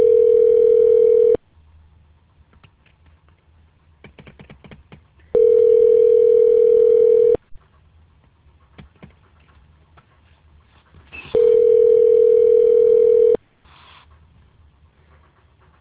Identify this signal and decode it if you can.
Ringback tone (North America), 440 + 480 Hz.